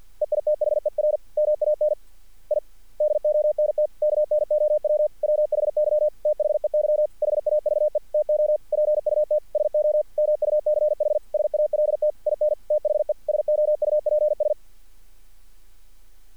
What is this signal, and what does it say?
Morse code, heard as a keyed tone, about 50 wpm, 600 Hz. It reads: EITHER MAN I DONT KNOW WHY THEY HAVE TO PUT SO MUCH SALT IN THE SOUPS
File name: archived / CRSnet